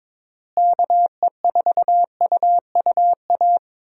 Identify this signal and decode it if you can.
Morse code, heard as a keyed tone, about 22 wpm, 700 Hz. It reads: KE4UUA
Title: USA Call Signs - 6 Characters - Repeated 2x - 22wpm